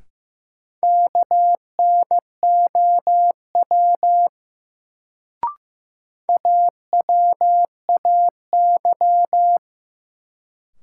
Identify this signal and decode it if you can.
Morse code, heard as a keyed tone, about 15 wpm, 700 Hz. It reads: KNOW AWAY
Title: Sets of 2 Words - New Word Drill - 15wpm